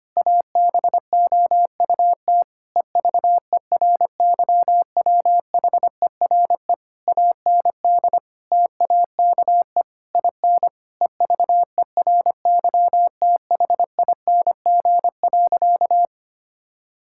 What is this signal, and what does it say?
Morse code, heard as a keyed tone, about 25 wpm, 700 Hz. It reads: ABOUT EVERYWHERE AND TAKE IN EVERYTHING.
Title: Sentences from Top 1600 Words 25wpm